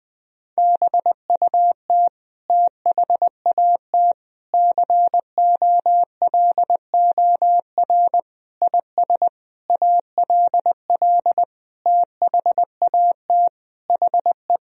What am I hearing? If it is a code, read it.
Morse code, heard as a keyed tone, about 21 wpm, 700 Hz. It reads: BUT THAT COLOR IS ALL THAT HE
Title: Sentences from Top 1000 Words - Review Part 2 - 20wpm